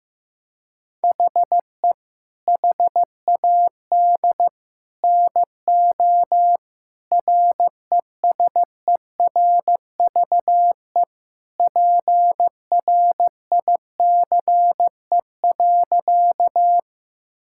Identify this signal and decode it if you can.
Morse code, heard as a keyed tone, about 15 wpm, 700 Hz. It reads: HE HAD NO RESERVE PRICE.
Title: Sentences from Top 1000 Words - Review Part 6 - 15wpm